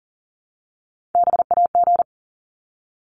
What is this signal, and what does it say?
Morse code, heard as a keyed tone, about 40 wpm, 700 Hz. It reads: BAG
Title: Top 1501-1750 Words 40wpm